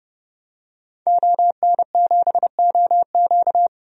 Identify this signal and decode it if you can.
Morse code, heard as a keyed tone, about 31 wpm, 700 Hz. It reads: ON7OQ
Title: DXCC-100 Call Signs - 5 Characters - 30wpm